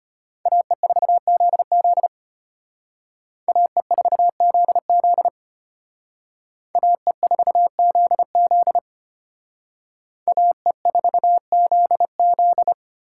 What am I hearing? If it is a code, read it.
Morse code, heard as a keyed tone, about 29 wpm, 700 Hz. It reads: AE4ZZ AE4ZZ AE4ZZ AE4ZZ
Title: Speed Racing - USA 2x2 Call Signs 25wpm